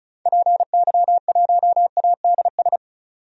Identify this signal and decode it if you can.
Morse code, heard as a keyed tone, about 36 wpm, 700 Hz. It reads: PY1ADS
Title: DXCC-100 Call Signs - 6 Characters - 35wpm